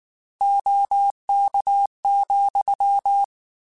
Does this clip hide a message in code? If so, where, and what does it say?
Yes: Morse code from 0.4 to 3.2 s, heard as a keyed tone, about 19 wpm, 800 Hz. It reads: OK,